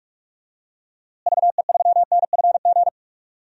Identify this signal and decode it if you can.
Morse code, heard as a keyed tone, about 41 wpm, 700 Hz. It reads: VE3NFG
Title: DXCC-100 Call Signs - 6 Characters - Repeated 2x - 45wpm